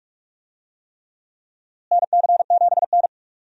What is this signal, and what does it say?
Morse code, heard as a keyed tone, about 47 wpm, 700 Hz. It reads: NC7N
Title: USA Call Signs - 4 Characters - Repeated 3x - 45wpm